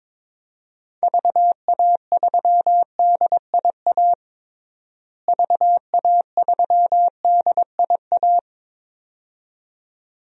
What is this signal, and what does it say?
Morse code, heard as a keyed tone, about 22 wpm, 700 Hz. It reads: VA3DIA VA3DIA
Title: DXCC-100 Call Signs - 6 Characters - Repeated 2x - 22wpm